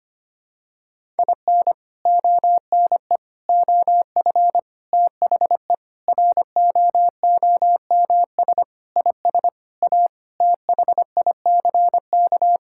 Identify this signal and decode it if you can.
Morse code, heard as a keyed tone, about 25 wpm, 700 Hz. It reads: IN ONE OF THE ROOMS IS A THICK
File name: practice / 25wpm / Sentences with 4-Word Phrases - 25wpm